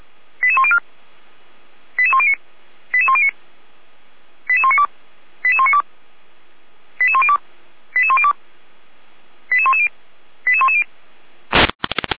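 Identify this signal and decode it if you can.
Five-tone selective call (ZVEI), heard as quick runs of five tones, tones of about 80 ms. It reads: 88104 88109 88109 88192 88192 88102 88102 88110 88110
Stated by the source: ZVEI five-tone selective call received on SDR, Feuerwehr Orlamünde.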